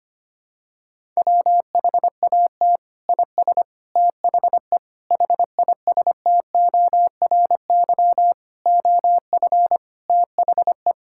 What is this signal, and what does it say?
Morse code, heard as a keyed tone, about 25 wpm, 700 Hz. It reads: WHAT IS THE HISTORY OF THE
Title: Sentences from Top 1000 Words - Review Part 5 - 25wpm